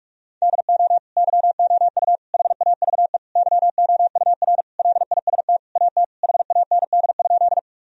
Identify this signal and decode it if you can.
Morse code, heard as a keyed tone, about 41 wpm, 700 Hz. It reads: DO YOU HAVE YOUR LIST AT HAND?
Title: Sentences from Top 1000 Words - Review Part 2 - 45wpm